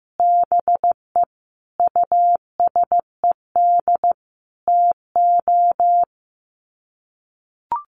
Morse code, heard as a keyed tone, about 15 wpm, 700 Hz. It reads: BE USED TO
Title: Sets of 5 Words Encore 15wpm